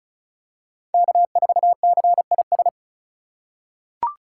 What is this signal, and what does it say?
Morse code, heard as a keyed tone, about 36 wpm, 700 Hz. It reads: K4CIS
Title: USA Call Signs - 3x Character Spacing - 35wpm